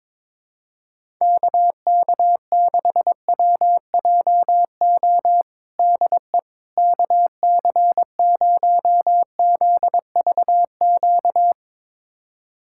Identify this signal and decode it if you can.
Morse code, heard as a keyed tone, about 22 wpm, 700 Hz. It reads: KK6WJO DE KC0ZVQ
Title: USA Callsign DE Callsign 22wpm